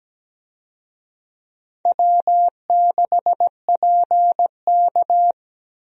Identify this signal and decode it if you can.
Morse code, heard as a keyed tone, about 17 wpm, 700 Hz. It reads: W6PK
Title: USA Call Signs - 4 Characters - Repeated 2x - 17wpm